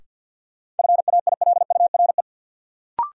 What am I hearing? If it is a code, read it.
Morse code, heard as a keyed tone, about 50 wpm, 700 Hz. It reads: FAILURE
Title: Sentences from Top 900 Words - New Word Drill - 50wpm